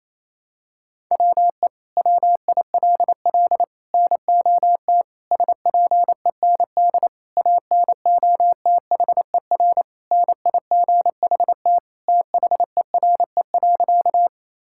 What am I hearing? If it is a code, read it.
Morse code, heard as a keyed tone, about 28 wpm, 700 Hz. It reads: WE WILL NOT SPEND ANOTHER NIGHT THERE.
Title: Sentences from Top 1000 Words - Review Part 5 - 28wpm